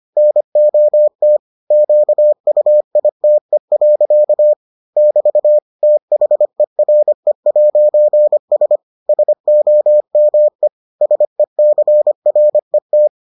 Morse code, heard as a keyed tone, about 25 wpm, 600 Hz. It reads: NOT QUITE. <BT> THERE'S SOME SECRET